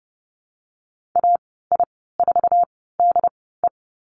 Morse code, heard as a keyed tone, about 30 wpm, spaced as for 17 wpm, 700 Hz. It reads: AI4DE
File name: Lesson 14g - Number 2 - Call Signs 30wpm